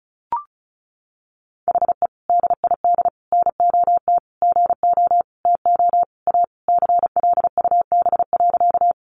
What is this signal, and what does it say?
Morse code, heard as a keyed tone, about 34 wpm, 700 Hz. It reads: HE DID NOT GO TO A CLUB.
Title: Sentences from Top 1200 Words 35wpm